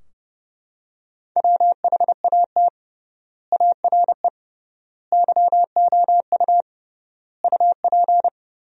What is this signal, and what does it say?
Morse code, heard as a keyed tone, about 30 wpm, 700 Hz. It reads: WHAT ARE YOU UP
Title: 3x Word-Spacing - Sentences from Top 100 Words - 30wpm